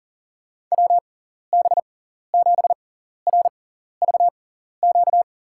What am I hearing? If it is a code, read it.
Morse code, heard as a keyed tone, about 40 wpm, spaced as for 14 wpm, 700 Hz. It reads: WB7RVQ
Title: USA Call Signs - 6 Characters - 6x Character Spacing - 40wpm